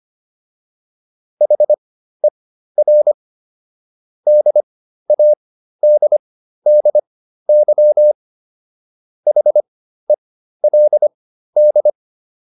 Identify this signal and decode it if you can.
Morse code, heard as a keyed tone, about 25 wpm, spaced as for 13 wpm, 600 Hz. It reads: HER DADDY HELD